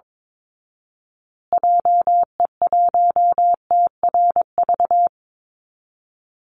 Morse code, heard as a keyed tone, about 22 wpm, 700 Hz. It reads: JE1TRV
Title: Call Signs Repeated 2x - 22wpm